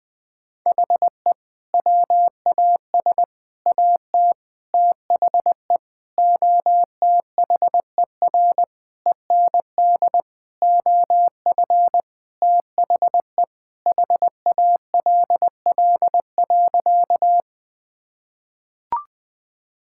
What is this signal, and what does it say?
Morse code, heard as a keyed tone, about 20 wpm, 700 Hz. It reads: HE WAS AT THE OTHER END OF THE HALL.
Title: Sentences with 5-Word Phrases - 20wpm